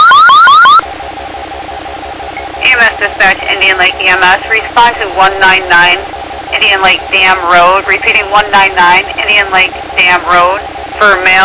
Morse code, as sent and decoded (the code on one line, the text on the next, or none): none